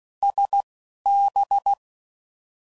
Morse code, heard as a keyed tone, about 16 wpm, 800 Hz. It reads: S B
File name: Mixed / Misc / Words